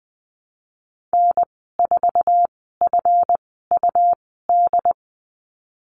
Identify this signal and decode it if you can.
Morse code, heard as a keyed tone, about 20 wpm, spaced as for 15 wpm, 700 Hz. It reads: N4FUD
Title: 140 Lesson 23g - Letter f - Call Signs 20wpm